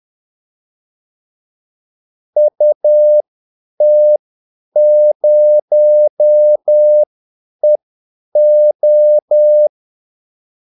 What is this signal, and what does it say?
Morse code, heard as a keyed tone, about 10 wpm, spaced as for 8 wpm, 600 Hz. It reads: UT0EO
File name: Callsigns-10-8wpm